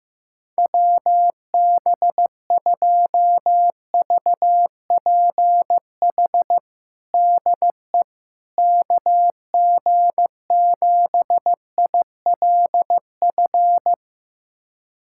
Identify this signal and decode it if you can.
Morse code, heard as a keyed tone, about 15 wpm, 700 Hz. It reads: WB2VPH DE KG7ILF
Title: USA Callsign DE Callsign 15wpm